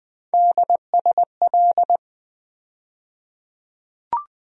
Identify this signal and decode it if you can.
Morse code, heard as a keyed tone, about 20 wpm, 700 Hz. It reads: DSL
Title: USA Call Sign Postfixes - 3 Letters 20wpm